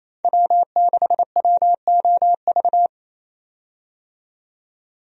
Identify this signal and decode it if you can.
Morse code, heard as a keyed tone, about 28 wpm, 700 Hz. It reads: W6WOV
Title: USA Call Signs - 5 Characters - 28wpm